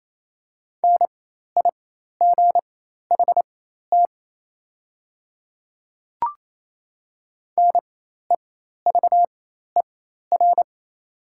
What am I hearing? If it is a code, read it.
Morse code, heard as a keyed tone, about 28 wpm, spaced as for 13 wpm, 700 Hz. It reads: NIGHT NEVER
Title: 4x Character Spacing - Top 300 Words - 28wpm